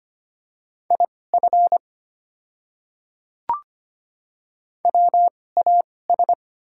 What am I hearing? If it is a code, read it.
Morse code, heard as a keyed tone, about 25 wpm, spaced as for 18 wpm, 700 Hz. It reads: IF WAS
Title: 2x Character Spacing - Top 200 Words - 25wpm